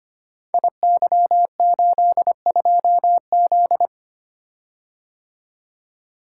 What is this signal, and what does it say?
Morse code, heard as a keyed tone, about 25 wpm, 700 Hz. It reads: IY82Z
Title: Code Groups - 5 Letter:Numbers - 25wpm